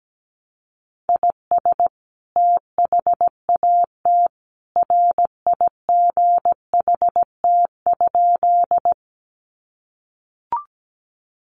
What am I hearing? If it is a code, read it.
Morse code, heard as a keyed tone, about 17 wpm, 700 Hz. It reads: IS THAT RIGHT?
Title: Sentences from Top 200 Words 17wpm